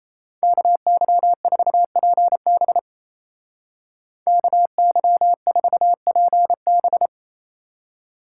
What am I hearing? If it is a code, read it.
Morse code, heard as a keyed tone, about 34 wpm, 700 Hz. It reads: KY4PB KY4PB
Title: Speed Racing - USA 2x2 Call Signs 28wpm